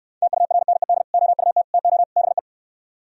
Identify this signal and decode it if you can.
Morse code, heard as a keyed tone, about 67 wpm, 700 Hz. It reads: TURNED OUT TO BE